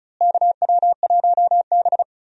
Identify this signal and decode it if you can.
Morse code, heard as a keyed tone, about 36 wpm, 700 Hz. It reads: KW1B
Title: USA Call Signs - 4 Characters - Repeated 3x - 35wpm